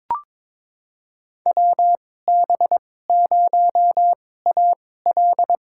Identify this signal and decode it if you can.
Morse code, heard as a keyed tone, about 22 wpm, spaced as for 16 wpm, 700 Hz. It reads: WB0AL
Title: USA Call Signs - 5 Characters - 2x Character Spacing - 22wpm